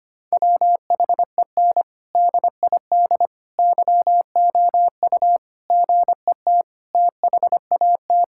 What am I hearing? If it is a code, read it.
Morse code, heard as a keyed tone, about 25 wpm, 700 Hz. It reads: WHEN DID YOU GET THAT
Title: Sentences from Top 1000 Words - Review Part 6 - 25wpm